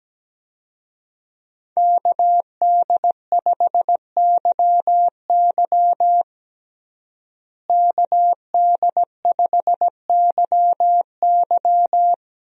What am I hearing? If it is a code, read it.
Morse code, heard as a keyed tone, about 17 wpm, 700 Hz. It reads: KD5YY KD5YY
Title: USA Call Signs - 5 Characters - Repeated 2x - 17wpm